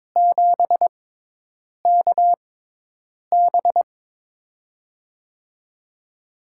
Morse code, heard as a keyed tone, about 22 wpm, spaced as for 8 wpm, 700 Hz. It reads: 7KB
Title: USA Call Signs - 4 Characters - 6x Character Spacing - 22wpm